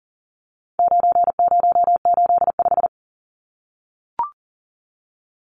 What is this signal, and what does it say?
Morse code, heard as a keyed tone, about 40 wpm, 700 Hz. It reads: 9085
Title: Numbers - 4-Digit 40wpm